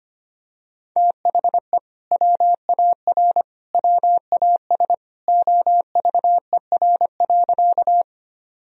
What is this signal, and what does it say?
Morse code, heard as a keyed tone, about 25 wpm, 700 Hz. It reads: THE WAR WAS OVER.
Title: Sentences from Top 1000 Words - Review Part 2 - 25wpm